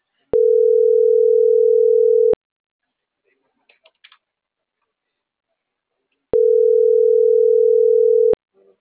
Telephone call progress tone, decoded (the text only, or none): ringback tone